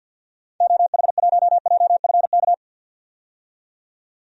Morse code, heard as a keyed tone, about 53 wpm, 700 Hz. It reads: OH1JFK
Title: DXCC-100 Call Signs - 6 Characters - 50wpm